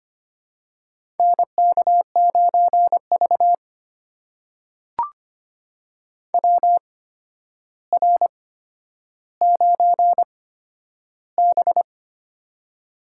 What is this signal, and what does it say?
Morse code, heard as a keyed tone, about 25 wpm, spaced as for 11 wpm, 700 Hz. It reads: NK9V WR9B
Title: USA Call Signs - 4 Characters - 8x Character Spacing - 25wpm